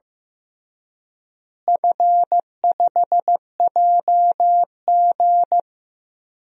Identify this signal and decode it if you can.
Morse code, heard as a keyed tone, about 15 wpm, 700 Hz. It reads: F5JG
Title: DXCC-100 Call Signs - 4 Characters - Repeated 2x - 15wpm